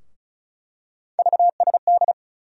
Morse code, heard as a keyed tone, about 34 wpm, 700 Hz. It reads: VSD
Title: USA Call Sign Postfixes - 3 Letters 35wpm